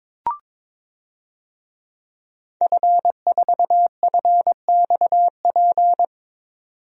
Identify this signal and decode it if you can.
Morse code, heard as a keyed tone, about 22 wpm, 700 Hz. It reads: F4FXP
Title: DXCC-100 Call Signs - 5 Characters - 22wpm